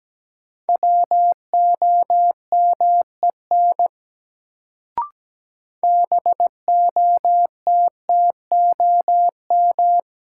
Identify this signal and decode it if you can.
Morse code, heard as a keyed tone, about 17 wpm, 700 Hz. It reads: WOMEN BOTTOM
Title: Sets of 3 Words - New Word Drill - 17wpm